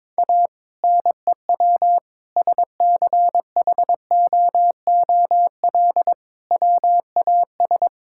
Morse code, heard as a keyed tone, about 22 wpm, 700 Hz. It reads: A NEW SCHOOL WAS
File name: practice / 22wpm / Sentences with 2-Word Phrases - 22wpm